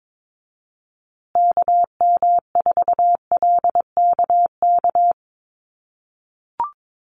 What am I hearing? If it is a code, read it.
Morse code, heard as a keyed tone, about 22 wpm, 700 Hz. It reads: KM4LKK
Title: USA Call Signs Repeated 2x 22wpm